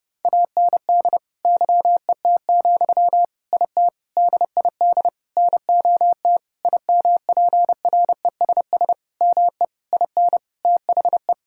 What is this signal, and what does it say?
Morse code, heard as a keyed tone, about 30 wpm, 700 Hz. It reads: AND YET, IT DID NOT IMPRESS ME IN THE
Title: Sentences from Top 1600 Words 30wpm